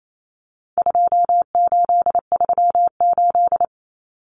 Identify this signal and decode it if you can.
Morse code, heard as a keyed tone, about 28 wpm, 700 Hz. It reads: 2838
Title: Common Cut Numbers - 4-Digit 28wpm